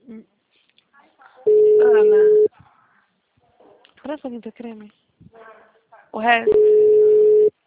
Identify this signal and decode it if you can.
Ringback tone (Europe, 425 Hz), with speech.
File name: out- -2001- - - .8399